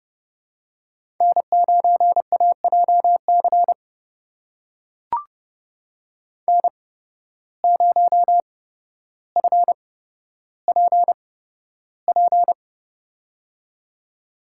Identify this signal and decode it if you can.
Morse code, heard as a keyed tone, about 30 wpm, spaced as for 13 wpm, 700 Hz. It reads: N9AJC N0FPP
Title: USA Call Signs - 5 Characters - 8x Character Spacing - 30wpm